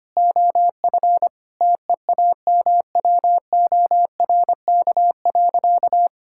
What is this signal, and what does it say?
Morse code, heard as a keyed tone, about 25 wpm, 700 Hz. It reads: OF TEAMWORK.
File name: practice / 25wpm / Difficult Sentences with Hints - 25wpm